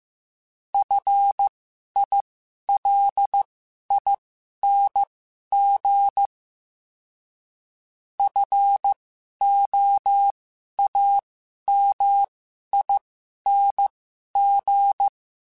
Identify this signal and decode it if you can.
Morse code, heard as a keyed tone, about 15 wpm, spaced as for 11 wpm, 800 Hz. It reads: FILING FOAMING
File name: Special INGs-mp3s / Special INGs-mp3s / Sngl Ing @10